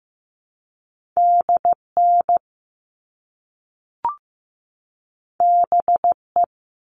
Morse code, heard as a keyed tone, about 15 wpm, 700 Hz. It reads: DN BE
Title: All Cut Numbers - 2-Digit 15wpm